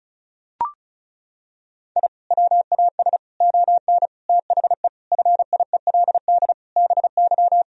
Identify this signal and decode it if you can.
Morse code, heard as a keyed tone, about 36 wpm, 700 Hz. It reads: I WAS ON THE FIELD BY
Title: Sentences from Top 1000 Words - Review Part 2 - 35wpm